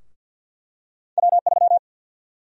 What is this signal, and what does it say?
Morse code, heard as a keyed tone, about 49 wpm, 700 Hz. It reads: W3
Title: USA Call Sign Prefixes - 50wpm